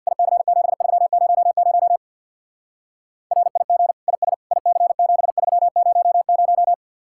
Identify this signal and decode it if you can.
Morse code, heard as a keyed tone, about 61 wpm, 700 Hz. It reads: IC7300 RIG IS IC7300
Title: Speed-Racing Common QSO Phrases Encore - 50wpm